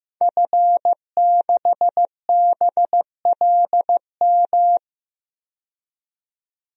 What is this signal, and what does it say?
Morse code, heard as a keyed tone, about 15 wpm, 700 Hz. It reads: F6BLM
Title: DXCC-100 Call Signs - 5 Characters - 15wpm